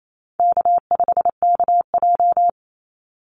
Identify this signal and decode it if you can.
Morse code, heard as a keyed tone, about 28 wpm, 700 Hz. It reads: K5KJ
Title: Call Signs - 28wpm